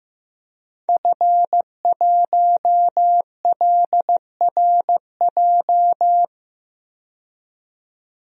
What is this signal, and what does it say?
Morse code, heard as a keyed tone, about 15 wpm, 700 Hz. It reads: F1LRJ
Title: DXCC-100 Call Signs - 5 Characters - 15wpm